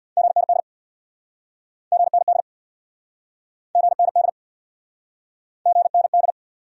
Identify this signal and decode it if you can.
Morse code, heard as a keyed tone, about 69 wpm, 700 Hz. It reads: GND GND GND GND